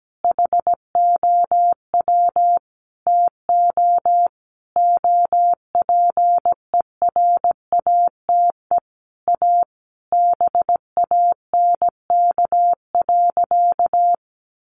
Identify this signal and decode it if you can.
Morse code, heard as a keyed tone, about 17 wpm, 700 Hz. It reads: HOW TO OPERATE A BANK.
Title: Sentences from Top 1500 Words 17wpm